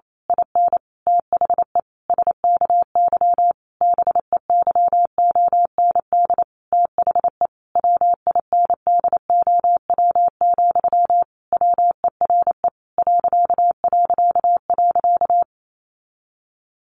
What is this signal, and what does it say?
Morse code, heard as a keyed tone, about 28 wpm, 700 Hz. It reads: IN THE SKY BEYOND THE WINDOW, WERE ...
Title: A Sound Of Thunder 28wpm